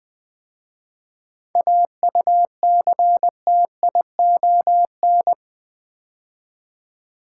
Morse code, heard as a keyed tone, about 20 wpm, 700 Hz. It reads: AUCTION
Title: 7 Letter Words - First 3 Letters Given - 20wpm